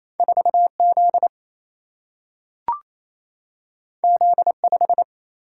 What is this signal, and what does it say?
Morse code, heard as a keyed tone, about 28 wpm, 700 Hz. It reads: 4 Z Z 5
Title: Single-Letter Single-Number 28wpm